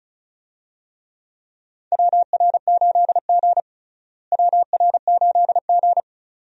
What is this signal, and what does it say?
Morse code, heard as a keyed tone, about 36 wpm, 700 Hz. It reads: WR8G WR8G
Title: USA Call Signs - 4 Characters - Repeated 2x - 35wpm